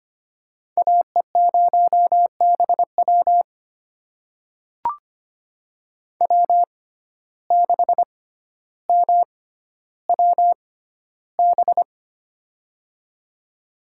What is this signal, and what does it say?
Morse code, heard as a keyed tone, about 25 wpm, spaced as for 13 wpm, 700 Hz. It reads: AE0BW W6MWB
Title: USA Call Signs - 5 Characters - 6x Character Spacing - 25wpm